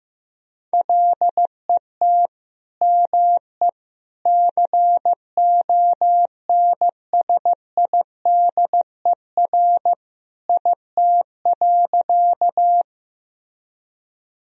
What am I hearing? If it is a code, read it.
Morse code, heard as a keyed tone, about 16 wpm, 700 Hz. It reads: LET ME CONSIDER IT.
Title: Sentences from Top 1000 Words - Review Part 4 - 15wpm